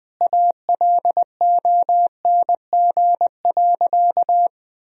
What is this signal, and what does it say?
Morse code, heard as a keyed tone, about 20 wpm, 700 Hz. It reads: ALONG.
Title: Sentences from Top 1000 Words - Review Part 3 - 20wpm